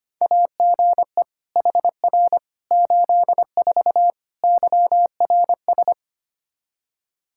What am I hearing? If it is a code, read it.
Morse code, heard as a keyed tone, about 25 wpm, 700 Hz. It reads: AGE HR 84 YRS
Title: Practice QSO - Exchange 3 - More Info - 25wpm